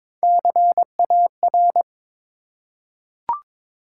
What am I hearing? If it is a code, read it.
Morse code, heard as a keyed tone, about 22 wpm, 700 Hz. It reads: CAR
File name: practice / 22wpm / Three Letter-Number Combinations - 22wpm